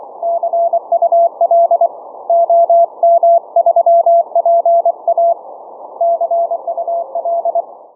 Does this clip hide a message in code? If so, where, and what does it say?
Yes: Morse code from 0.2 to 7.6 s, heard as a keyed tone, about 24 wpm, 680 Hz. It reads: CUL OM3PA CUL